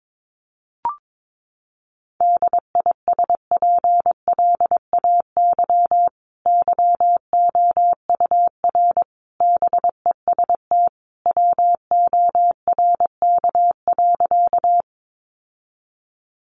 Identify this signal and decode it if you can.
Morse code, heard as a keyed tone, about 22 wpm, 700 Hz. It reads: DISPLAY YOUR BEST WORK.
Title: Sentences from Top 1200 Words 22wpm